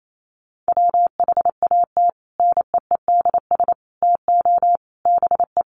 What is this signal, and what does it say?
Morse code, heard as a keyed tone, about 28 wpm, 700 Hz. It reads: WHAT NEEDS TO BE
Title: Sets of 5 Words Encore 28wpm